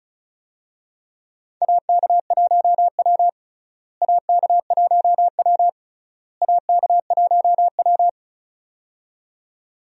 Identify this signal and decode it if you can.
Morse code, heard as a keyed tone, about 36 wpm, 700 Hz. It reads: AK1W AK1W AK1W
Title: USA Call Signs - 4 Characters - Repeated 3x - 35wpm